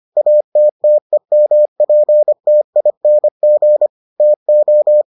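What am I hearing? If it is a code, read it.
Morse code, heard as a keyed tone, about 25 wpm, 600 Hz. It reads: ATTEMPTING TO
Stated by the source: Morse edition of the Foundations of Amateur Radio episodes.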